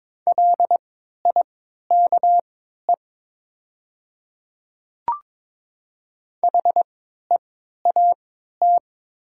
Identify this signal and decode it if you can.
Morse code, heard as a keyed tone, about 22 wpm, spaced as for 12 wpm, 700 Hz. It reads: LIKE HEAT
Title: 3x Character Spacing - Top 500 Words - 22wpm